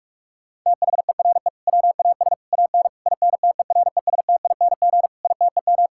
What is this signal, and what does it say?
Morse code, heard as a keyed tone, about 46 wpm, 700 Hz. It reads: THERE WAS AN INTERESTING ITEM
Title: Sentences from Top 1000 Words - Review Part 6 - 45wpm